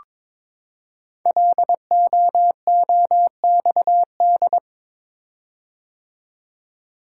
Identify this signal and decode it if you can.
Morse code, heard as a keyed tone, about 22 wpm, 700 Hz. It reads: LOOXD
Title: Code Groups - 5 Letter:Numbers - 22wpm